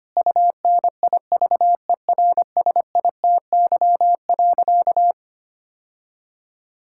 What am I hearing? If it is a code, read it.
Morse code, heard as a keyed tone, about 25 wpm, 700 Hz. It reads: UNIVERSITY.
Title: Sentences from Top 1000 Words - Review Part 5 - 25wpm